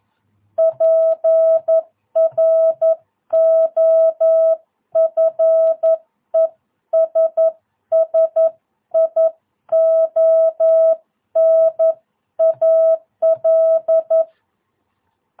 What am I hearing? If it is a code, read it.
Morse code, heard as a keyed tone, about 11 wpm, 650 Hz. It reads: PROFESSIONAL
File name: jh8cbh / 2022cw